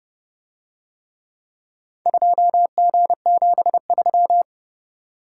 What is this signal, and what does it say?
Morse code, heard as a keyed tone, about 29 wpm, 700 Hz. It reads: 2G73